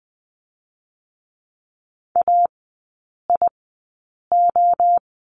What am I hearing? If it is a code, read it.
Morse code, heard as a keyed tone, about 20 wpm, spaced as for 8 wpm, 700 Hz. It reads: AIO
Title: 107 Lesson 18e - Letter m - Three Characters 20wpm